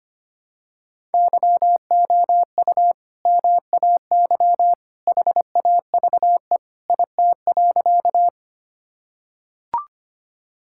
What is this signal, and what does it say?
Morse code, heard as a keyed tone, about 25 wpm, 690 Hz. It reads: YOU MAY HAVE IT.
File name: practice / 25wpm / Sentences with 2-Word Phrases - 25wpm